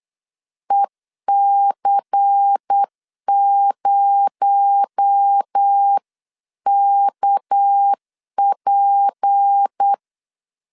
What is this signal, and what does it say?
Morse code, heard as a keyed tone, about 8 wpm, 800 Hz. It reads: EC0KP
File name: CW / DET / 1 6 1500C S B 8wpm 5142